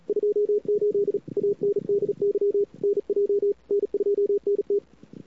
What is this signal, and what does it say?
Morse code, heard as a keyed tone, about 43 wpm, 410 Hz. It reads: 28 ANDY NJ N2NT
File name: CWSprint / Feb2007 / 2007-feb-cw-sprint-k5tr-op-n5rz -00:00